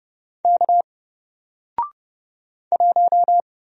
Morse code, heard as a keyed tone, about 30 wpm, 700 Hz. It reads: K 1
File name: Single-Letter-Number-30wpm